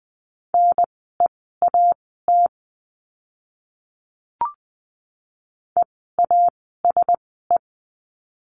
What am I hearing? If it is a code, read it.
Morse code, heard as a keyed tone, about 20 wpm, spaced as for 15 wpm, 700 Hz. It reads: NEAT EASE
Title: 054 Lesson 10f - Letter r - Words 20wpm